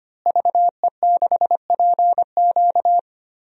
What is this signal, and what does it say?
Morse code, heard as a keyed tone, about 25 wpm, 700 Hz. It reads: VE6PQ
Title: DXCC-100 Call Signs - 5 Characters - Repeated 3x - 25wpm